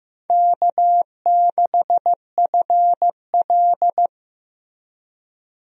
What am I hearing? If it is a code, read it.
Morse code, heard as a keyed tone, about 16 wpm, 700 Hz. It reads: K6FL
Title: USA Call Signs - 4 Characters - Repeated 3x - 15wpm